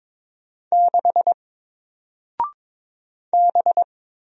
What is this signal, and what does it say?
Morse code, heard as a keyed tone, about 22 wpm, 700 Hz. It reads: 6 B
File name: Single-Letter-Number-22wpm